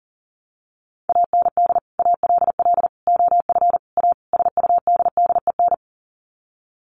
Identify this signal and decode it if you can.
Morse code, heard as a keyed tone, about 40 wpm, 700 Hz. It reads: AND ALL OF A SUDDEN
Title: Sets of 5 Words 40wpm